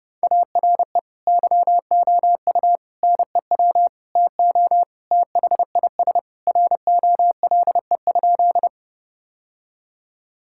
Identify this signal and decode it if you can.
Morse code, heard as a keyed tone, about 30 wpm, 700 Hz. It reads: ARE YOU NEW TO THIS ROLE?
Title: Sentences from Top 1000 Words - Review Part 6 - 30wpm